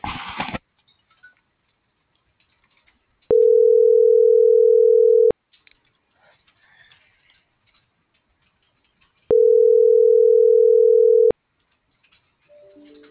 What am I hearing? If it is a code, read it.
Ringback tone (North America), 440 + 480 Hz.